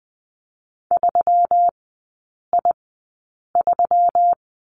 Morse code, heard as a keyed tone, about 20 wpm, spaced as for 8 wpm, 700 Hz. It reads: 3I3
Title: 121 Lesson 20e - Number 3 - Three Characters 20wpm